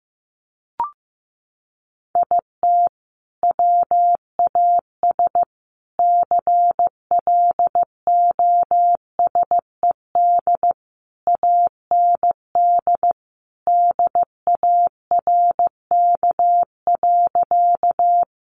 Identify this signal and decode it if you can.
Morse code, heard as a keyed tone, about 15 wpm, 700 Hz. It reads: IT WAS CLOSED AND DARK.
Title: Sentences from Top 1400 Words 15wpm